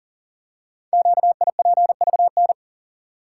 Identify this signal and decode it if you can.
Morse code, heard as a keyed tone, about 40 wpm, 700 Hz. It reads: QIPVN
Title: Code Groups - 5 Letter:Numbers - 40wpm